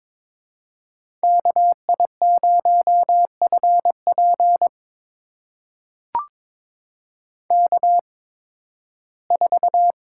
Morse code, heard as a keyed tone, about 22 wpm, spaced as for 14 wpm, 700 Hz. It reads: KI0FP K4
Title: USA Call Signs - 5 Characters - 8x Character Spacing - 22wpm